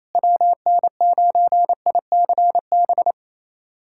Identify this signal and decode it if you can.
Morse code, heard as a keyed tone, about 28 wpm, 700 Hz. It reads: WN9ICB
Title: USA Callsign DE Callsign 28wpm